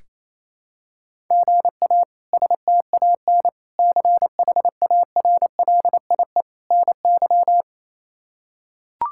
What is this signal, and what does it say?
Morse code, heard as a keyed tone, about 28 wpm, 700 Hz. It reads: GA STAN CHARLIE NY
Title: K1USN SST QSO Practice - 28wpm